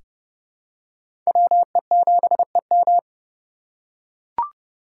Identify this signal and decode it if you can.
Morse code, heard as a keyed tone, about 30 wpm, 700 Hz. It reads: WE7EM
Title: USA Call Signs - 5 Characters - 4x Character Spacing - 30wpm